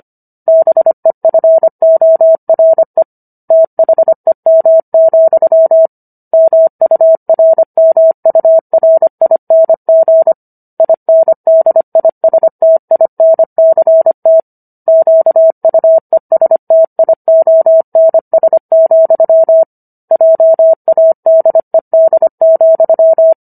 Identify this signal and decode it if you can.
Morse code, heard as a keyed tone, about 25 wpm, 650 Hz. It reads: BEFORE THEM, MURMURING INDISTINCT QUESTIONS, JADED,